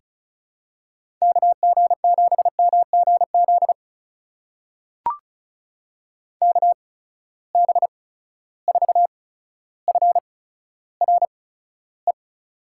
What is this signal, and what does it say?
Morse code, heard as a keyed tone, about 36 wpm, spaced as for 15 wpm, 700 Hz. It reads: KG7MGZ KB4FRE
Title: USA Call Signs - 6 Characters - 8x Character Spacing - 35wpm